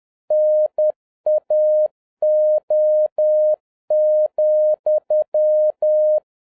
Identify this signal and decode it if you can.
Morse code, heard as a keyed tone, about 10 wpm, 610 Hz. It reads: NAO,
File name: 2011-07-mp3 / n36 THE 3.11 2 10wpm